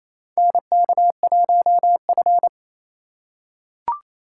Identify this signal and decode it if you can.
Morse code, heard as a keyed tone, about 28 wpm, 700 Hz. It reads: NK1F
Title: USA Call Signs - 3x Character Spacing - 28wpm